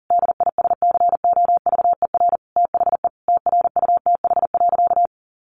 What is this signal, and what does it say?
Morse code, heard as a keyed tone, about 37 wpm, 700 Hz. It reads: DISCOVER THE TRUTH.
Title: Sentences from Top 1100 Words 40wpm